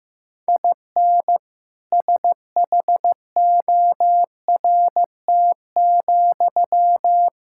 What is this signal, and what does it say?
Morse code, heard as a keyed tone, about 15 wpm, 700 Hz. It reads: IN SHORT,
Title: Sentences from Top 1000 Words - Review Part 5 - 15wpm